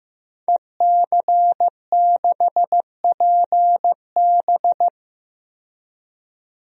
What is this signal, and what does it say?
Morse code, heard as a keyed tone, about 15 wpm, 700 Hz. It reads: EC6PB
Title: DXCC-100 Call Signs - 5 Characters - 15wpm